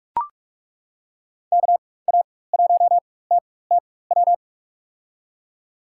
Morse code, heard as a keyed tone, about 42 wpm, spaced as for 20 wpm, 700 Hz. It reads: KA1TTW
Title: USA Call Signs - 6 Characters - 4x Character Spacing - 45wpm